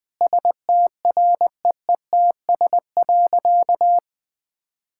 Morse code, heard as a keyed tone, about 20 wpm, 700 Hz. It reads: STREETS.
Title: Easy Sentences with Hints - 20wpm